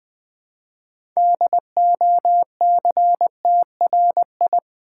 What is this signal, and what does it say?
Morse code, heard as a keyed tone, about 20 wpm, 700 Hz. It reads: DOCTRI
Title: Build A Word - 2 Letters At A Time - 8-Letter Words - 20wpm